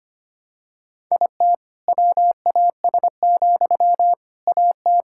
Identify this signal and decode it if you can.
Morse code, heard as a keyed tone, about 25 wpm, 700 Hz. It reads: IT WAS, AT
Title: Sentences from Top 1000 Words - Review Part 6 - 25wpm